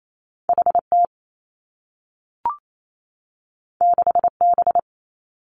Morse code, heard as a keyed tone, about 28 wpm, 700 Hz. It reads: H T 6 B
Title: Two Challenging Letter-Number Combinations 28wpm